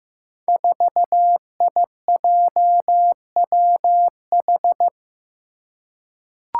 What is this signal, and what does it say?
Morse code, heard as a keyed tone, about 15 wpm, 700 Hz. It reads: 4IJWH
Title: Five Challenging Letter-Number Combinations 15wpm